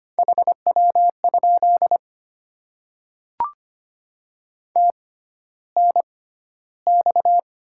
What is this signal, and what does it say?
Morse code, heard as a keyed tone, about 25 wpm, spaced as for 13 wpm, 700 Hz. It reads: HW? TNX
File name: Top 20 QSO Elements - 6x Character Spacing - 25wpm